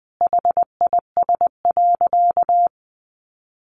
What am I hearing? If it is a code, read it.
Morse code, heard as a keyed tone, about 20 wpm, 700 Hz. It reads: HIS.
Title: Sentences from Top 900 Words 20wpm